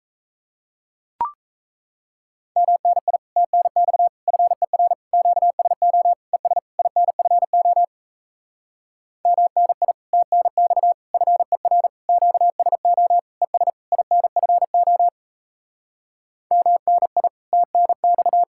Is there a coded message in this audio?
Yes: Morse code from 2.6 to 18.4 s, heard as a keyed tone, about 37 wpm, 700 Hz. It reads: MNI TNX FER QSO ES INFO MNI TNX FER QSO ES INFO MNI TNX